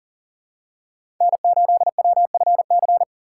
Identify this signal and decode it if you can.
Morse code, heard as a keyed tone, about 40 wpm, 700 Hz. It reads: N8WFC
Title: USA Call Signs - 5 Characters - 4x Character Spacing - 40wpm